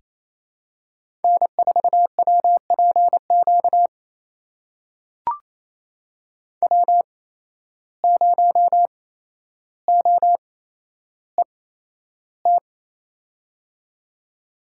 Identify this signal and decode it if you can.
Morse code, heard as a keyed tone, about 28 wpm, spaced as for 12 wpm, 700 Hz. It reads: N4WPQ W0OET
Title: USA Call Signs - 5 Characters - 8x Character Spacing - 28wpm